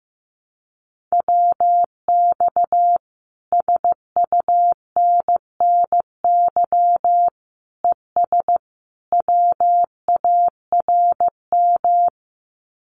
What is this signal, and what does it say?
Morse code, heard as a keyed tone, about 15 wpm, 700 Hz. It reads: WX SUNNY ES WARM
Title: Common QSO Phrases Encore 15wpm